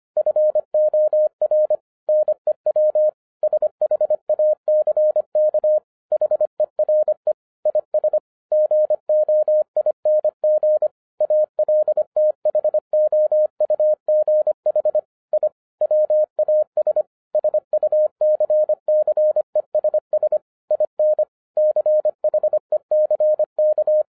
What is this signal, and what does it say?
Morse code, heard as a keyed tone, about 25 wpm, 610 Hz. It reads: FOR NEW SHACK HERE IS GOING ALTHOUGH I WAS SUCCESS IN CHECK
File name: 2011-10-mp3 / n04-JUST MOVED 25wpm